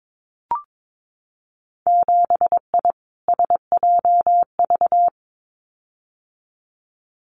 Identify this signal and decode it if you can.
Morse code, heard as a keyed tone, about 22 wpm, 700 Hz. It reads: 7I SJV
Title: Winter Field Day Exchanges Encore 22wpm